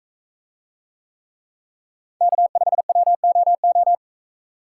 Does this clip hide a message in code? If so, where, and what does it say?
Yes: Morse code from 2.2 to 4.0 s, heard as a keyed tone, about 42 wpm, 700 Hz. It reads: K5WOO